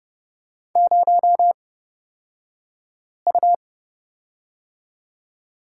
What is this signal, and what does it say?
Morse code, heard as a keyed tone, about 31 wpm, spaced as for 5 wpm, 700 Hz. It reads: 0U
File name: Lesson 38d - Letter q - Two Characters 30wpm